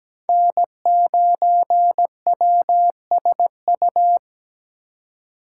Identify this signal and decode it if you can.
Morse code, heard as a keyed tone, about 17 wpm, 700 Hz. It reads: N9WSU
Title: USA Call Signs - 5 Characters - 3x Character Spacing - 17wpm